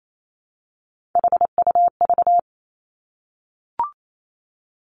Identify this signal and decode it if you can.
Morse code, heard as a keyed tone, about 28 wpm, 700 Hz. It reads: HUV
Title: Three Challenging Letter-Number Combinations 28wpm